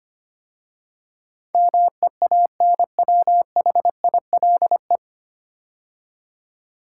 Morse code, heard as a keyed tone, about 25 wpm, 700 Hz. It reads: MEANWHILE
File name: practice / 25wpm / 9 Letter Words - First 3 Letters Given - 25wpm